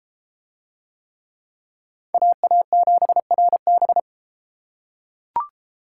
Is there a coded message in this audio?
Yes: Morse code from 2.1 to 4.0 s, heard as a keyed tone, about 28 wpm, 700 Hz. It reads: AA7RB